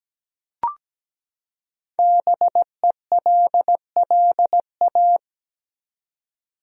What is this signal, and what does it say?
Morse code, heard as a keyed tone, about 17 wpm, 700 Hz. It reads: BELLA
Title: USA Common Names - 17wpm